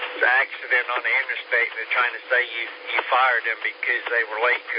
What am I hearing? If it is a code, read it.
Emergency Alert System (SAME) tones, heard as harsh buzzing bursts, with speech. It is heard from 1.0 to 4.0 s.